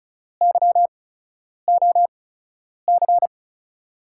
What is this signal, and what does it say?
Morse code, heard as a keyed tone, about 34 wpm, spaced as for 10 wpm, 700 Hz. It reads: YOC
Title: USA Call Signs - 5 Characters - 8x Character Spacing - 35wpm